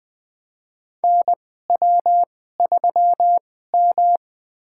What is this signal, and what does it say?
Morse code, heard as a keyed tone, about 20 wpm, spaced as for 14 wpm, 700 Hz. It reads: NW3M
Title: USA Call Signs - 4 Characters - 2x Character Spacing - 20wpm